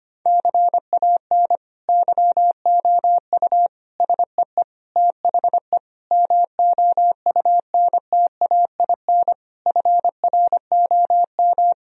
Morse code, heard as a keyed tone, about 25 wpm, 700 Hz. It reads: CAN YOU SEE THE MOUNTAIN FROM
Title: Sentences from Top 1000 Words - Review Part 1 - 25wpm